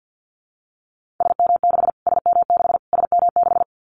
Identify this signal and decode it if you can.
Morse code, heard as a keyed tone, about 50 wpm, 700 Hz. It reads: SM6 SM6 SM6
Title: Callsign Prefixes for Top 25 DXCC - Repeated 3x - 50wpm